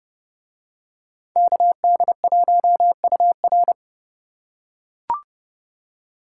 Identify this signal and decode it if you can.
Morse code, heard as a keyed tone, about 30 wpm, 700 Hz. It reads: KD1UR